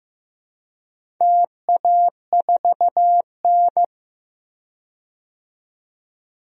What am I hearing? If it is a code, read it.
Morse code, heard as a keyed tone, about 15 wpm, 700 Hz. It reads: TA4N
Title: DXCC-100 Call Signs - 4 Characters - 15wpm